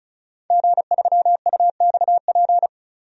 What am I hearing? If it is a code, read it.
Morse code, heard as a keyed tone, about 34 wpm, 700 Hz. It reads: G3UXP
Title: DXCC-100 Call Signs - 5 Characters - Repeated 2x - 35wpm